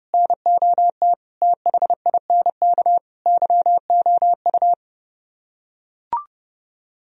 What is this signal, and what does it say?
Morse code, heard as a keyed tone, about 29 wpm, 700 Hz. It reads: NOT THINK YOU